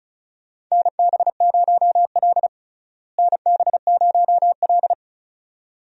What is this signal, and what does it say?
Morse code, heard as a keyed tone, about 36 wpm, 700 Hz. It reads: NB0L NB0L
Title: USA Call Signs - 4 Characters - Repeated 2x - 35wpm